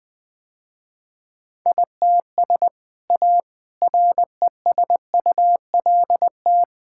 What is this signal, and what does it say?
Morse code, heard as a keyed tone, about 20 wpm, 700 Hz. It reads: ITS A RESULT
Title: Sentences with 3-Word Phrases - Spoken Beginning - 20wpm